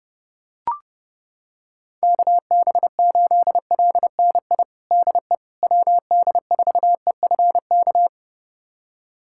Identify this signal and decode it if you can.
Morse code, heard as a keyed tone, about 32 wpm, 700 Hz. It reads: KB8LNI DE WD4EFK
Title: USA Callsign DE Callsign 30wpm